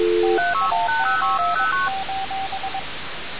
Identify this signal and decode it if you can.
Morse code, heard as a keyed tone, about 22 wpm, 770 Hz, with noise. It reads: TIME:
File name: radiocap / REC z